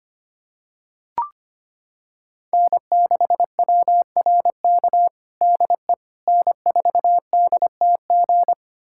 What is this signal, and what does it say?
Morse code, heard as a keyed tone, about 25 wpm, 700 Hz. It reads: N6WRK DE N4DTG
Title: USA Callsign DE Callsign 25wpm